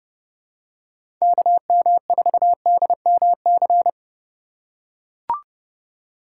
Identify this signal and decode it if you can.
Morse code, heard as a keyed tone, about 30 wpm, 700 Hz. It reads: KM4DMC